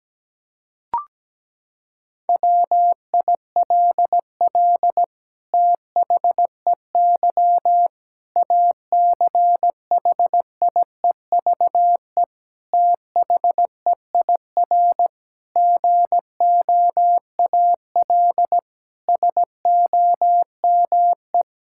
Morse code, heard as a keyed tone, about 17 wpm, 700 Hz. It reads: WILL THEY ACHIEVE THEIR GOAL SOME
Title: Sentences from Top 1000 Words - Review Part 6 - 17wpm